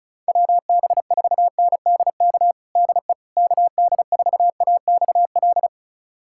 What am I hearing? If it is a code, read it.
Morse code, heard as a keyed tone, about 36 wpm, 700 Hz. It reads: WB4NDK DE KD4AXL
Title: USA Callsign DE Callsign - 6 Characters - 35wpm